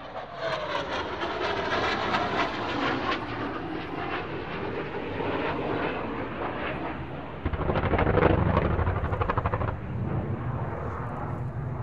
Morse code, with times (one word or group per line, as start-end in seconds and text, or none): none